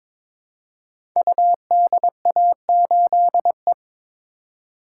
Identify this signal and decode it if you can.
Morse code, heard as a keyed tone, about 23 wpm, 700 Hz. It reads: UDA8E
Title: Code Groups - 5 Letter:Numbers - 22wpm